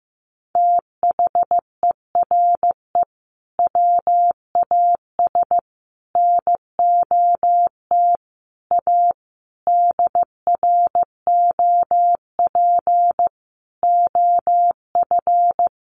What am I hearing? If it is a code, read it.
Morse code, heard as a keyed tone, about 15 wpm, 700 Hz. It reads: THERE WAS NOT A DROP OF
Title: Sentences from Top 1500 Words 15wpm